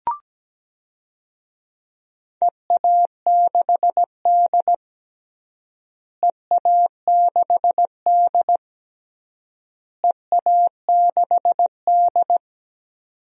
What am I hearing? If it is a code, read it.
Morse code, heard as a keyed tone, about 17 wpm, 700 Hz. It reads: EA6D EA6D EA6D
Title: DXCC-100 Call Signs - 4 Characters - Repeated 3x - 17wpm